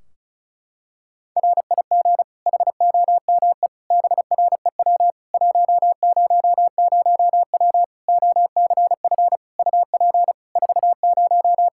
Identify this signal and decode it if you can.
Morse code, heard as a keyed tone, about 34 wpm, 700 Hz. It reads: RIG HOME BREW 100W OCF UP 40
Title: K1USN SST QSO Practice 35wpm